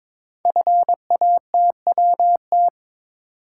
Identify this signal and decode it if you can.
Morse code, heard as a keyed tone, about 23 wpm, 700 Hz. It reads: FATWT
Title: Code Groups - 5 Letters - 22wpm